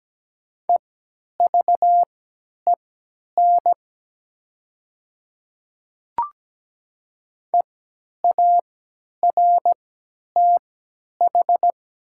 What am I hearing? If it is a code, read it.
Morse code, heard as a keyed tone, about 17 wpm, spaced as for 10 wpm, 700 Hz. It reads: EVEN EARTH
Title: 3x Character Spacing - Top 300 Words - 17wpm